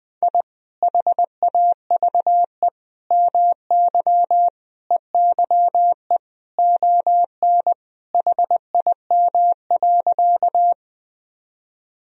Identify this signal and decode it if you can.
Morse code, heard as a keyed tone, about 20 wpm, 700 Hz. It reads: I HAVE MY EYE ON HIM.
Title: Sentences from Top 1000 Words - Review Part 1 - 20wpm